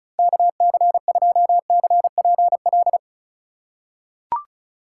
Morse code, heard as a keyed tone, about 36 wpm, 700 Hz. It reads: KC2CPL
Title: USA Call Signs - 6 Characters - 2x Character Spacing - 35wpm